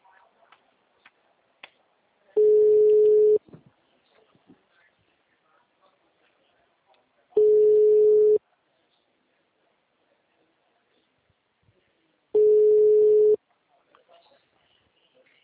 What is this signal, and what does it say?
Ringback tone (Europe, 425 Hz), with speech.